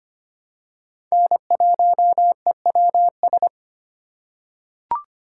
Morse code, heard as a keyed tone, about 25 wpm, 700 Hz. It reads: N1EWS
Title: USA Call Signs - 5 Characters - 4x Character Spacing - 25wpm